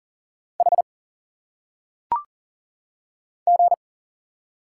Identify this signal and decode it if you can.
Morse code, heard as a keyed tone, about 40 wpm, 700 Hz. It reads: H G
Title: Single Letters - Phonetic Pronunciation 40wpm